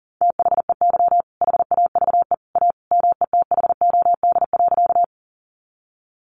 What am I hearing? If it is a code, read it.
Morse code, heard as a keyed tone, about 37 wpm, 700 Hz. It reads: THEY HAVE A METHOD.
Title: Sentences from Top 600 Words 40wpm